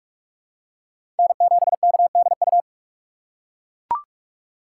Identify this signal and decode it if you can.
Morse code, heard as a keyed tone, about 41 wpm, 690 Hz. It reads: N7KDU
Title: USA Call Signs - 5 Characters - 2x Character Spacing - 45wpm